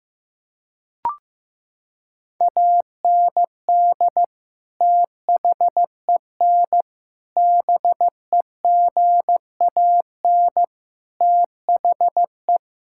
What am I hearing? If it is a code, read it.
Morse code, heard as a keyed tone, about 15 wpm, 700 Hz. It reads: AND THEN BEGAN THE
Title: Sentences from Top 1000 Words - Review Part 5 - 15wpm